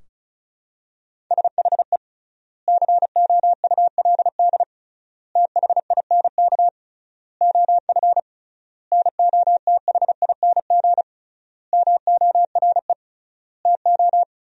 Morse code, heard as a keyed tone, about 34 wpm, 700 Hz. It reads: SHE COULD THINK OF NOTHING MORE TO
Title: 3x Word-Spacing - Sentences from Top 500 Words - 35wpm